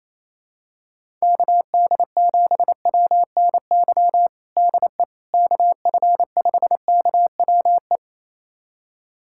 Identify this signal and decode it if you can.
Morse code, heard as a keyed tone, about 28 wpm, 700 Hz. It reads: KD7WNY DE KF5KWE
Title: USA Callsign DE Callsign - 6 Characters - 28wpm